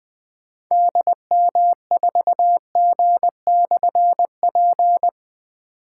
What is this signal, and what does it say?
Morse code, heard as a keyed tone, about 21 wpm, 700 Hz. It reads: DM4G/P
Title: Difficult Call Signs 20wpm